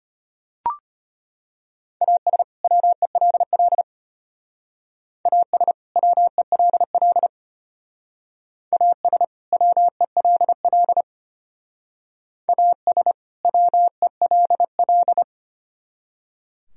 Morse code, heard as a keyed tone, about 33 wpm, 700 Hz. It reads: AS WELL AS WELL AS WELL AS WELL